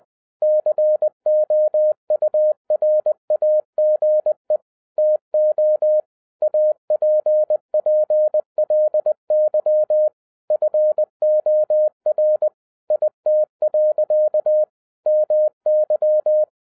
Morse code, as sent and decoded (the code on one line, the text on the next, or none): -.-. --- ..- .-. .- --. / - --- / .- .--. .--. .-.. -.-- / ..-. --- .-. / .. - .-.-.- / -- -.--
COURAGE TO APPLY FOR IT. MY